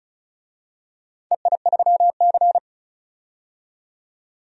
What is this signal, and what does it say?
Morse code, heard as a keyed tone, about 36 wpm, 700 Hz. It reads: EI3C
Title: DXCC-100 Call Signs - 4 Characters - 35wpm